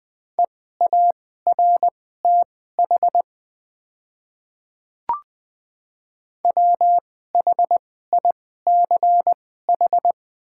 Morse code, heard as a keyed tone, about 20 wpm, spaced as for 14 wpm, 700 Hz. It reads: EARTH WHICH
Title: 2x Character Spacing - Top 200 Words - 20wpm